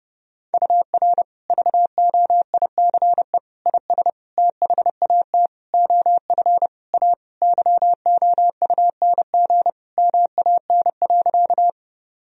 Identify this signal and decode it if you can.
Morse code, heard as a keyed tone, about 30 wpm, 700 Hz. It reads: UR VOICE IS THAT OF A YOUNG MAN.
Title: Sentences from Top 500 Words and Abbreviations 30wpm